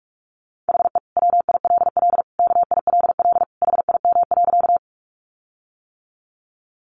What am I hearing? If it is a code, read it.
Morse code, heard as a keyed tone, about 44 wpm, 700 Hz. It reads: HE WILL KILL HIM.
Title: Sentences from Top 1400 Words 45wpm